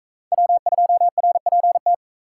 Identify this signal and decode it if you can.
Morse code, heard as a keyed tone, about 39 wpm, 700 Hz. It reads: W2RPT